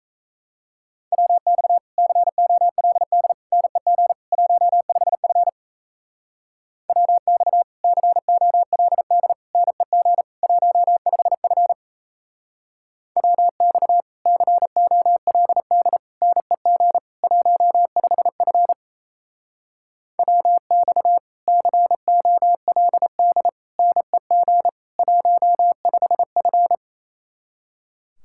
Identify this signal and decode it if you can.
Morse code, heard as a keyed tone, about 34 wpm, 700 Hz. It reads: WX COLD NEG 15F WX COLD NEG 15F WX COLD NEG 15F WX COLD NEG 15F
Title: Speed-Racing Common QSO Phrases - 28wpm